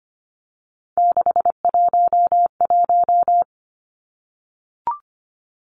Morse code, heard as a keyed tone, about 25 wpm, 700 Hz. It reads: 611
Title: Three Challenging Letter-Number Combinations 25wpm